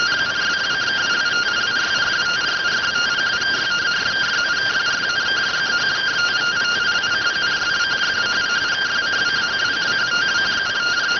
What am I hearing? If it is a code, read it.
Two-tone teleprinter signal, about 170 Hz shift.